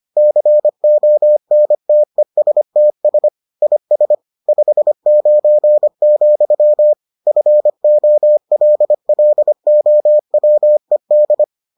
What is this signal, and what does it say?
Morse code, heard as a keyed tone, about 25 wpm, 600 Hz. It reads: CONTESTS IS 59, FOLLOWED